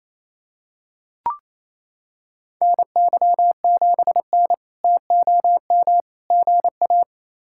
Morse code, heard as a keyed tone, about 28 wpm, 700 Hz. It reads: NY7N TOM GA
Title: K1USN SST QSO Practice - 28wpm